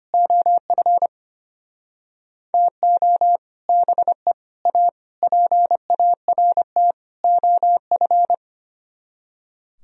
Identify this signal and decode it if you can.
Morse code, heard as a keyed tone, about 30 wpm, 700 Hz. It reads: OF TO BE A PART OF
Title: Speed-Racing - Sets of 5 Words 25wpm